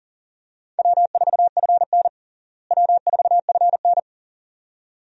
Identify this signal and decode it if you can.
Morse code, heard as a keyed tone, about 37 wpm, 700 Hz. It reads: W4FN W4FN
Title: USA Call Signs - 4 Characters - Repeated 2x - 40wpm